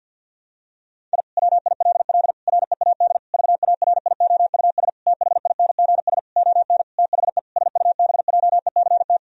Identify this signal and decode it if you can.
Morse code, heard as a keyed tone, about 50 wpm, 700 Hz. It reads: I WILL READ VARIOUS THINGS ON THE SUBJECT
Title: Sentences from Top 1000 Words - Review Part 4 - 50wpm